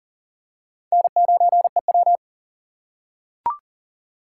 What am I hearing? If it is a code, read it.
Morse code, heard as a keyed tone, about 37 wpm, 700 Hz. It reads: N9EW
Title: USA Call Signs - 4 Characters - 2x Character Spacing - 40wpm